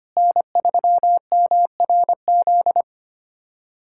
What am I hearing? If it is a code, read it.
Morse code, heard as a keyed tone, about 25 wpm, 700 Hz. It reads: N3MRZ
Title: USA Callsign DE Callsign - 5 Characters - 25wpm